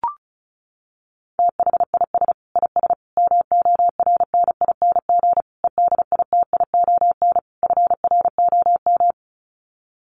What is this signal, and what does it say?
Morse code, heard as a keyed tone, about 34 wpm, 700 Hz. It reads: THIS IS MORNING EDITION FROM
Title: Sets of 5 Words Encore2 35wpm